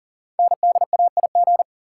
Morse code, heard as a keyed tone, about 41 wpm, 690 Hz. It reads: NDAIG
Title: Code Groups - 5 Letters - 40wpm